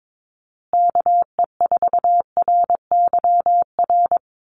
Morse code, heard as a keyed tone, about 22 wpm, 700 Hz. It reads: KE4RYR
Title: USA Call Signs Repeated 3x 22wpm